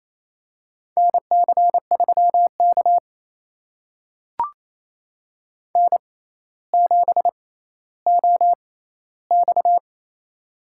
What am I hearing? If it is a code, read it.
Morse code, heard as a keyed tone, about 28 wpm, spaced as for 14 wpm, 700 Hz. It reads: NC3K N7OX
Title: USA Call Signs - 4 Characters - 6x Character Spacing - 28wpm